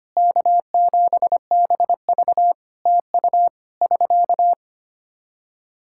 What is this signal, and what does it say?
Morse code, heard as a keyed tone, about 25 wpm, 700 Hz. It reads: K7BV TU <SK>
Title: Practice QSO - Exchange 4 - Ending - 25wpm